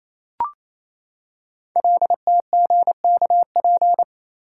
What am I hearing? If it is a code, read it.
Morse code, heard as a keyed tone, about 27 wpm, 690 Hz. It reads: LTGKP
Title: Code Groups - 5 Letter:Numbers - 28wpm